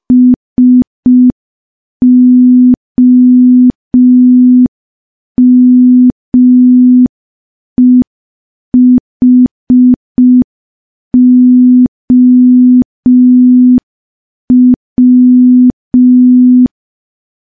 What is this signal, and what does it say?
Morse code, heard as a keyed tone, about 5 wpm, 250 Hz. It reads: SOMEHOW
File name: morse / morse-training-05WPM